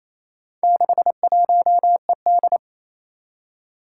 Morse code, heard as a keyed tone, about 27 wpm, 700 Hz. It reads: 61ED
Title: Four Challenging Letter-Number Combinations 28wpm